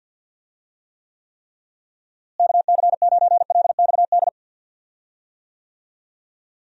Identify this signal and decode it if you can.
Morse code, heard as a keyed tone, about 47 wpm, 700 Hz. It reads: KC9LXD
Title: USA Call Signs - 6 Characters - Repeated 3x - 50wpm